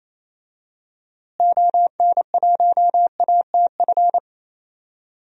Morse code, heard as a keyed tone, about 28 wpm, 700 Hz. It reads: ON1ATF
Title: DXCC-100 Call Signs - 6 Characters - Repeated 3x - 28wpm